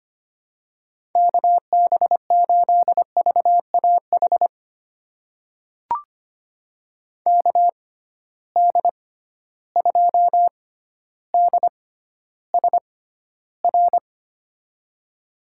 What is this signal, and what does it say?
Morse code, heard as a keyed tone, about 25 wpm, spaced as for 13 wpm, 700 Hz. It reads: KB8VAH KD2DSR
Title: USA Call Signs - 6 Characters - 6x Character Spacing - 25wpm